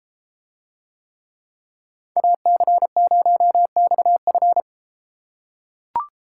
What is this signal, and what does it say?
Morse code, heard as a keyed tone, about 33 wpm, 700 Hz. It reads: AC0XF